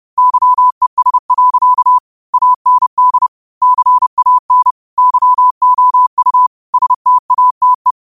Morse code, heard as a keyed tone, about 30 wpm, 1000 Hz. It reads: YES. AND CAN YOU STATE
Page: New quotes every day in morse code at 30 Words per minute.